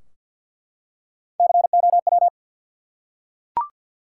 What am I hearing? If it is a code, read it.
Morse code, heard as a keyed tone, about 51 wpm, 700 Hz. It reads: COW
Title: Three Letter Words - 50wpm